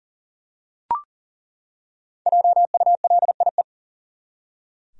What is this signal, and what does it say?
Morse code, heard as a keyed tone, about 37 wpm, 700 Hz. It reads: JULIE
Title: USA Common Names - Max 5 Letters - 40wpm